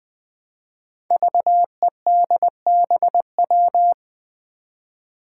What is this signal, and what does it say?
Morse code, heard as a keyed tone, about 20 wpm, 700 Hz. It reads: VEDBW
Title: Five Challenging Letter-Number Combinations 20wpm